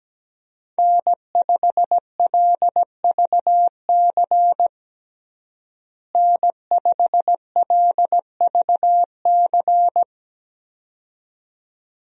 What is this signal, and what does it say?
Morse code, heard as a keyed tone, about 17 wpm, 700 Hz. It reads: N5LVC N5LVC
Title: USA Call Signs - 5 Characters - Repeated 2x - 17wpm